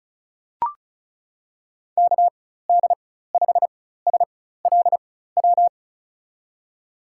Morse code, heard as a keyed tone, about 34 wpm, spaced as for 16 wpm, 700 Hz. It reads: KD5SLW
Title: USA Call Signs - 6 Characters - 4x Character Spacing - 35wpm